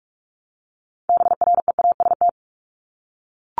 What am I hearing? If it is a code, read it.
Morse code, heard as a keyed tone, about 44 wpm, 700 Hz. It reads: BREAST
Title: Top 1751-2000 Words 45wpm